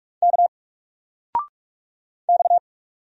Morse code, heard as a keyed tone, about 44 wpm, 700 Hz. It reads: K X
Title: Single Letter-Number 45wpm